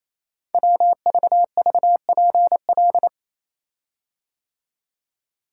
Morse code, heard as a keyed tone, about 29 wpm, 690 Hz. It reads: WVVPL
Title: Code Groups - 5 Letter:Numbers - 28wpm